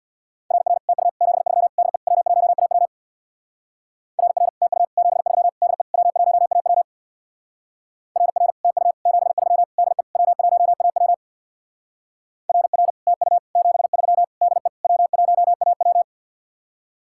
Morse code, heard as a keyed tone, about 61 wpm, 700 Hz. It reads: RR TU 73 DE W1AW RR TU 73 DE W1AW RR TU 73 DE W1AW RR TU 73 DE W1AW
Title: Speed-Racing Common QSO Phrases - 50wpm